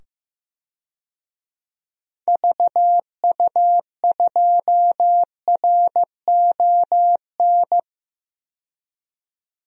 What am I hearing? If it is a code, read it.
Morse code, heard as a keyed tone, about 16 wpm, 700 Hz. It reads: VU2RON
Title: DXCC-100 Call Signs - 6 Characters - 15wpm